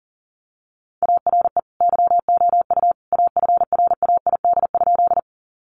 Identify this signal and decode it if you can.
Morse code, heard as a keyed tone, about 40 wpm, 700 Hz. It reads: ARE YOU AFRAID?
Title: Sentences from Top 1500 Words 40wpm